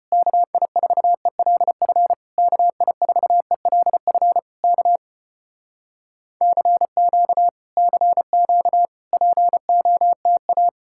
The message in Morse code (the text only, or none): KI4ELF KI4ELF K CQ CQ POTA